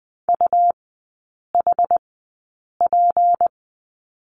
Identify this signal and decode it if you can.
Morse code, heard as a keyed tone, about 20 wpm, spaced as for 8 wpm, 700 Hz. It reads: UHP
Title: 159 Lesson 26e - Letter g - Three Characters 20wpm